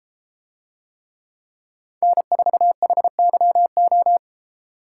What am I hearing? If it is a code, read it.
Morse code, heard as a keyed tone, about 33 wpm, 700 Hz. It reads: N4HYO